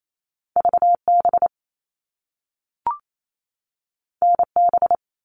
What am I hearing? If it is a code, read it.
Morse code, heard as a keyed tone, about 28 wpm, 700 Hz. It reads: VB NB
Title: All Cut Numbers - 2-Digit 28wpm